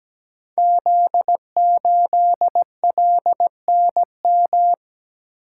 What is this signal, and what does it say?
Morse code, heard as a keyed tone, about 17 wpm, 700 Hz. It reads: Z8LNM
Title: Code Groups - 5 Letter:Numbers - 17wpm